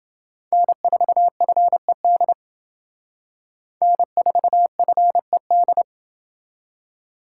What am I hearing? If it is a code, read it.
Morse code, heard as a keyed tone, about 29 wpm, 700 Hz. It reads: N4FED N4FED